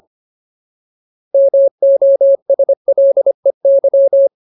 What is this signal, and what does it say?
Morse code, heard as a keyed tone, about 25 wpm, 550 Hz. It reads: MOSLEY
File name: Ultimate Word List / 25WPM Ultimate